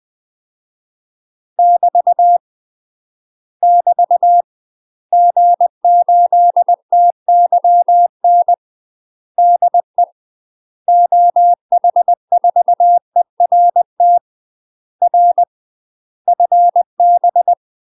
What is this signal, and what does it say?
Morse code, heard as a keyed tone, about 20 wpm, 700 Hz. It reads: <BT> <BT> G8TYN DE OH4ERT R FB
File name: UK MASTER QSOs TRAINING / 20WPM / UK MASTER QSOs 0011